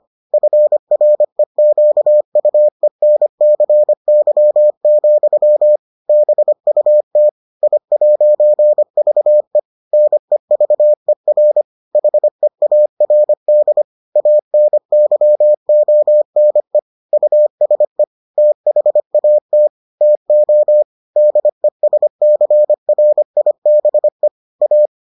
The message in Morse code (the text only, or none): FREQUENCY, BUT I'VE NEVER HEARD ANYONE USE THAT TO DESCRIBE A